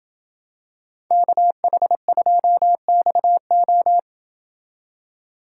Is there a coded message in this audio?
Yes: Morse code from 1.1 to 4.0 s, heard as a keyed tone, about 26 wpm, 700 Hz. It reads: KH2XO